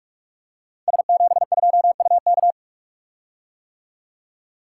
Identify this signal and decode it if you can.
Morse code, heard as a keyed tone, about 44 wpm, 700 Hz. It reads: S72UK
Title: Code Groups - 5 Letter:Numbers - 45wpm